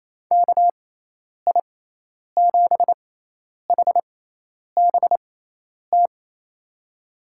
Morse code, heard as a keyed tone, about 27 wpm, spaced as for 10 wpm, 700 Hz. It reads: KI7HBT
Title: USA Call Signs - 6 Characters - 6x Character Spacing - 28wpm